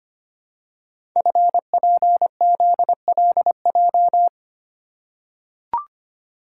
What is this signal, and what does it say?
Morse code, heard as a keyed tone, about 25 wpm, 700 Hz. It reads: FPZLJ
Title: Code Groups - 5 Letter:Numbers - 25wpm